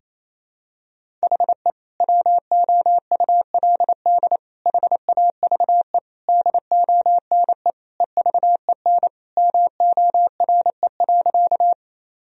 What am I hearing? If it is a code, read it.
Morse code, heard as a keyed tone, about 28 wpm, 700 Hz. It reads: HE WOULD HAVE DONE EVEN MORE.
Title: Sentences from Top 1000 Words - Review Part 1 - 28wpm